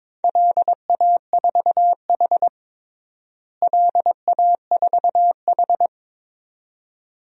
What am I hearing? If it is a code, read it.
Morse code, heard as a keyed tone, about 22 wpm, 700 Hz. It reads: LA4H LA4H
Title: DXCC-100 Call Signs - 4 Characters - Repeated 2x - 22wpm